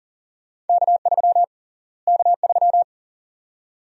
Morse code, heard as a keyed tone, about 40 wpm, 700 Hz. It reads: K3 K3
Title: USA Call Sign Prefixes - Repeated 2x - 40wpm